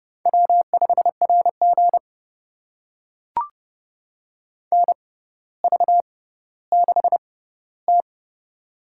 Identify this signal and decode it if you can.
Morse code, heard as a keyed tone, about 32 wpm, spaced as for 16 wpm, 700 Hz. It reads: W5RG NV6T
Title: USA Call Signs - 4 Characters - 6x Character Spacing - 30wpm